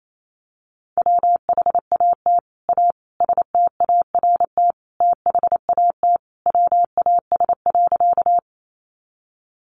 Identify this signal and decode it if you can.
Morse code, heard as a keyed tone, about 28 wpm, 700 Hz. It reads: WHAT A START THAT WAS.
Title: Sentences from Top 300 Words 28wpm